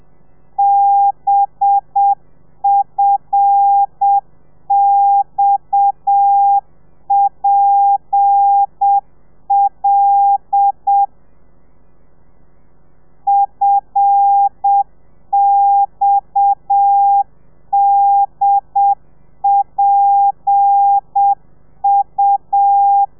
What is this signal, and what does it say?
Morse code, heard as a keyed tone, about 7 wpm, 800 Hz. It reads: BFXPL FXDPU
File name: telegrafia / i2tfj / Lez7 30